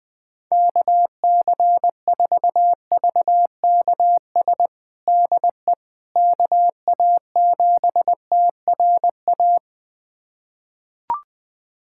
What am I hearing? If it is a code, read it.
Morse code, heard as a keyed tone, about 20 wpm, 700 Hz. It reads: KC4VKS DE KA7TRA
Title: USA Callsign DE Callsign 20wpm